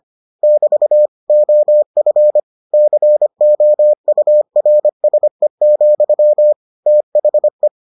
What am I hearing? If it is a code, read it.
Morse code, heard as a keyed tone, about 25 wpm, 600 Hz. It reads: <BT> OF COURSE, THE